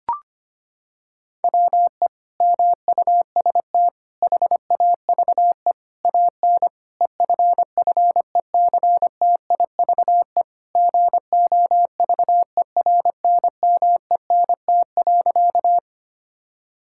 Morse code, heard as a keyed tone, about 25 wpm, 700 Hz. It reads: WE MUST HAVE AN EFFECTIVE GOVERNMENT.
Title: Sentences from Top 1000 Words - Review Part 6 - 25wpm